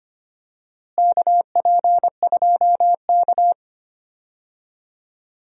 Morse code, heard as a keyed tone, about 25 wpm, 700 Hz. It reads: KP2K
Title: DXCC-100 Call Signs - 4 Characters - Repeated 2x - 25wpm